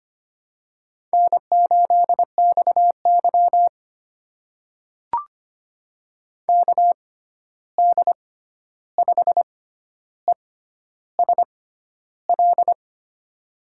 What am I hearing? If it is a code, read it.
Morse code, heard as a keyed tone, about 25 wpm, spaced as for 11 wpm, 700 Hz. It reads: N8XY KD5ESL
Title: USA Call Signs - 6x Character Spacing - 25wpm